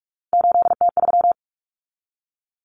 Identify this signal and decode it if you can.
Morse code, heard as a keyed tone, about 44 wpm, 700 Hz. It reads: 8T3
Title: Common Cut Numbers - 3-Digit 45wpm